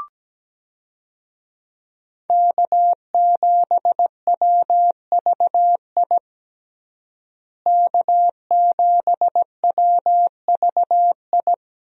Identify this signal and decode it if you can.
Morse code, heard as a keyed tone, about 17 wpm, 700 Hz. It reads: K7WVI K7WVI
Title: USA Call Signs - 5 Characters - Repeated 2x - 17wpm